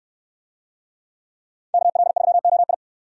Morse code, heard as a keyed tone, about 68 wpm, 700 Hz. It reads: KL3OI